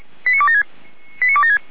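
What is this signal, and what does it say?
Five-tone selective call (ZVEI), heard as quick runs of five tones, tones of about 80 ms. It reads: 89276
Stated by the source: ZVEI five-tone selective call received with an SDR.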